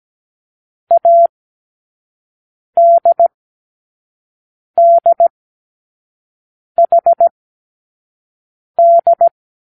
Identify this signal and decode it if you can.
Morse code, heard as a keyed tone, about 17 wpm, spaced as for 5 wpm, 680 Hz. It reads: ADDHD
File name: k7qo / Lesson18(A-L)